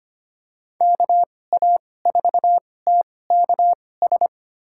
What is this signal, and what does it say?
Morse code, heard as a keyed tone, about 25 wpm, spaced as for 18 wpm, 700 Hz. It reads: KA4TKS
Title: USA Call Signs - 6 Characters - 2x Character Spacing - 25wpm